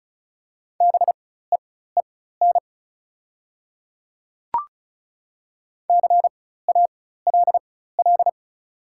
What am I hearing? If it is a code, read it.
Morse code, heard as a keyed tone, about 36 wpm, spaced as for 16 wpm, 700 Hz. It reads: BEEN CALL
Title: 4x Character Spacing - Top 100 Words - 35wpm